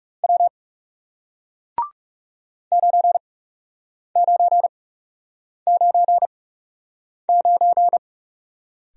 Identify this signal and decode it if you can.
Morse code, heard as a keyed tone, about 40 wpm, spaced as for 8 wpm, 700 Hz. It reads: W 9999